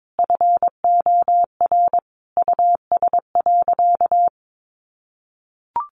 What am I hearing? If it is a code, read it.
Morse code, heard as a keyed tone, about 22 wpm, 700 Hz. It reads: FOR US.
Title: Sentences from Top 1500 Words 22wpm